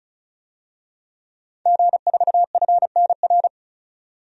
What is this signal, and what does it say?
Morse code, heard as a keyed tone, about 34 wpm, 700 Hz. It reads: G4FNR
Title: DXCC-100 Call Signs - 5 Characters - 35wpm